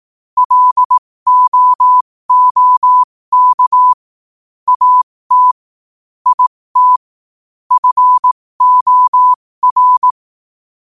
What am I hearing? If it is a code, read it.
Morse code, heard as a keyed tone, about 18 wpm, spaced as for 15 wpm, 1000 Hz. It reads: LOOK AT IT FOR
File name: qotdfiles / 15wpm